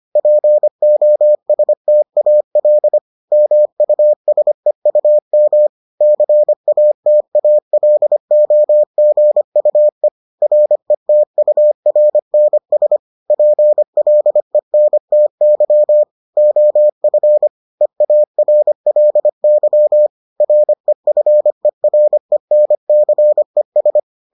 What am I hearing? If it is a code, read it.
Morse code, heard as a keyed tone, about 25 wpm, 600 Hz. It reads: POSTAL MUSEUM CATALOGUE RETURNS PLENTY OF EARLY REFERENCES